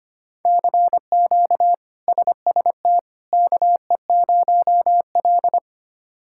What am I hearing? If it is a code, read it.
Morse code, heard as a keyed tone, about 25 wpm, 700 Hz. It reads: CQ SST KE0L
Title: K1USN SST QSO Practice - 25wpm